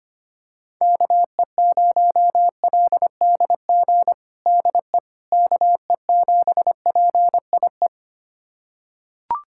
Morse code, heard as a keyed tone, about 25 wpm, 700 Hz. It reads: KE0LDG DE KE7PIE
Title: USA Callsign DE Callsign - 6 Characters - 25wpm